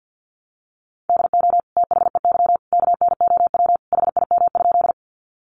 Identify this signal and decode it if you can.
Morse code, heard as a keyed tone, about 50 wpm, 700 Hz. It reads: DO THEY KNOW HIM?
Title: Sentences from Top 100 Words 50wpm